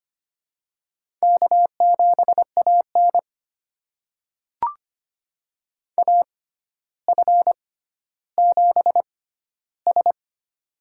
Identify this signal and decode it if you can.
Morse code, heard as a keyed tone, about 25 wpm, spaced as for 13 wpm, 700 Hz. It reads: K7AN AF7S
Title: USA Call Signs - 4 Characters - 6x Character Spacing - 25wpm